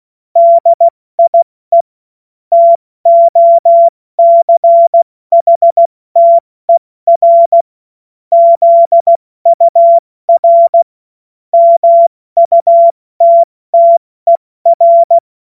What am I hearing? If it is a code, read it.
Morse code, heard as a keyed tone, about 16 wpm, 680 Hz. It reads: DIE TOCHTER ZUR MUTTER
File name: Morsekurs / Witze / Witz 14. März 2023 70 F80